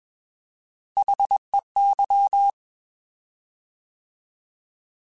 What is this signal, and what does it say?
Morse code, heard as a keyed tone, about 21 wpm, 800 Hz. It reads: HEY